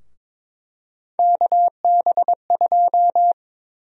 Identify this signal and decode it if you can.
Morse code, heard as a keyed tone, about 22 wpm, 700 Hz. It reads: KB2
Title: USA Call Sign Prefixes - 22wpm